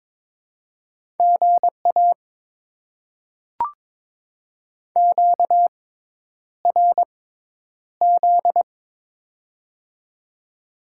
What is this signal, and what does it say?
Morse code, heard as a keyed tone, about 22 wpm, spaced as for 10 wpm, 700 Hz. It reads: GA QRZ
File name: Top 90 QSO Elements - 6x Character Spacing - 22wpm